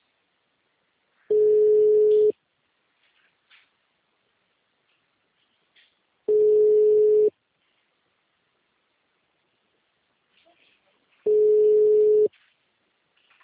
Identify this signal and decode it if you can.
Ringback tone (Europe, 425 Hz).